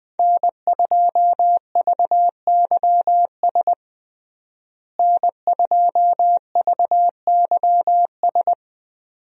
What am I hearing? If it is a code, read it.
Morse code, heard as a keyed tone, about 20 wpm, 700 Hz. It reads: N2VYS N2VYS
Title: USA Call Signs - 5 Characters - Repeated 2x - 20wpm